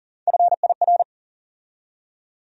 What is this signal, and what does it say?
Morse code, heard as a keyed tone, about 40 wpm, 700 Hz. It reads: FIR
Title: USA Call Sign Postfixes - 3 Letters 40wpm